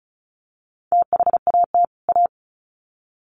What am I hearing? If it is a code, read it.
Morse code, heard as a keyed tone, about 34 wpm, 700 Hz. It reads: THAT A
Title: Sets of 2 Words 35wpm